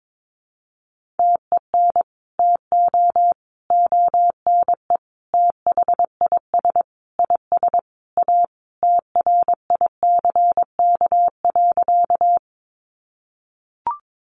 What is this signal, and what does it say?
Morse code, heard as a keyed tone, about 22 wpm, 700 Hz. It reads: TEN TO ONE THIS IS A TRICK.
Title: Sentences from Top 1500 Words 22wpm